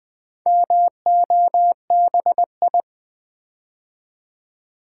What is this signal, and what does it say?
Morse code, heard as a keyed tone, about 20 wpm, 700 Hz. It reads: MOBI
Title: Build A Word - 4 Letters At A Time - 8-Letter Words - 20wpm